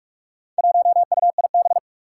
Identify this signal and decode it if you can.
Morse code, heard as a keyed tone, about 44 wpm, 700 Hz. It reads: 1UIB
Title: Four Challenging Letter-Number Combinations 45wpm